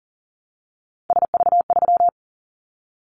Morse code, heard as a keyed tone, about 40 wpm, 700 Hz. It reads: SV3
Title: Callsign Prefixes for Top 50 DXCC - 40wpm